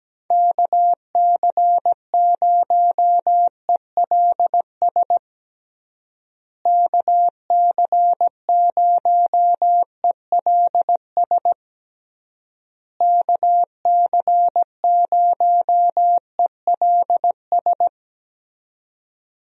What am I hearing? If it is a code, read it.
Morse code, heard as a keyed tone, about 17 wpm, 700 Hz. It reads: KC0ELS KC0ELS KC0ELS
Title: USA Call Signs - 6 Characters - Repeated 3x - 17wpm